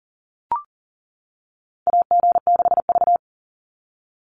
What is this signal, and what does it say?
Morse code, heard as a keyed tone, about 40 wpm, 700 Hz. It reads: AG6V
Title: NA SOTA Chaser Callsigns 40wpm